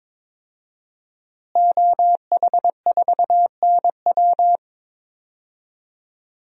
Morse code, heard as a keyed tone, about 22 wpm, 700 Hz. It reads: OH4NW
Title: DXCC-100 Call Signs - 5 Characters - Repeated 3x - 22wpm